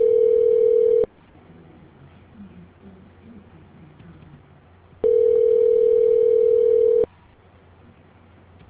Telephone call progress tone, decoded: ringback tone

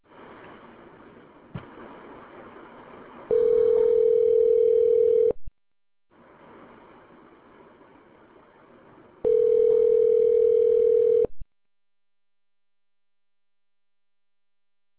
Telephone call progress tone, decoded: ringback tone

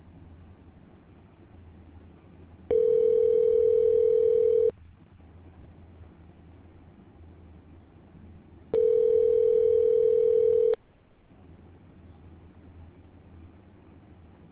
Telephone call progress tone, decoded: ringback tone

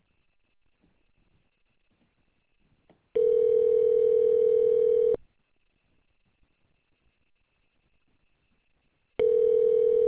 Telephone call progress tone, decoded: ringback tone

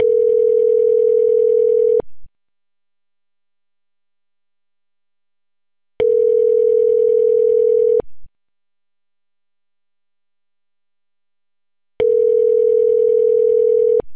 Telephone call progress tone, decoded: ringback tone

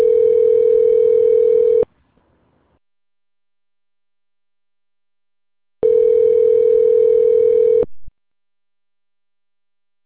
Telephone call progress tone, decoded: ringback tone

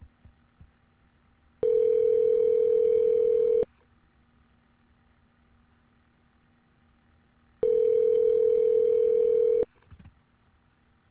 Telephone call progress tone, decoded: ringback tone

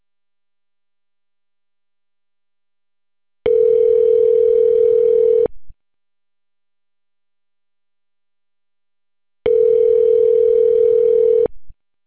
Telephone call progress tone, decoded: ringback tone

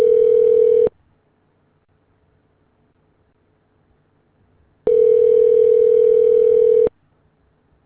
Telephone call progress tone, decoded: ringback tone

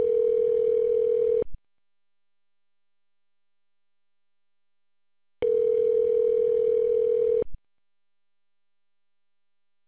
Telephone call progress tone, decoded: ringback tone

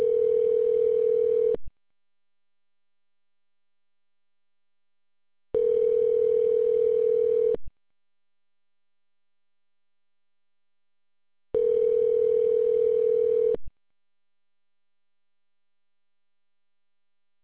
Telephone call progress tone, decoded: ringback tone

